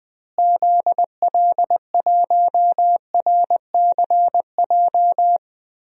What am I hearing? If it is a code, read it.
Morse code, heard as a keyed tone, about 21 wpm, 700 Hz. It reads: ZL1RCJ
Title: DXCC-100 Call Signs - 6 Characters - Repeated 2x - 20wpm